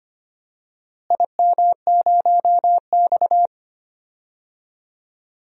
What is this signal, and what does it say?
Morse code, heard as a keyed tone, about 25 wpm, 700 Hz. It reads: IM0X
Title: DXCC-100 Call Signs - 4 Characters - Repeated 2x - 25wpm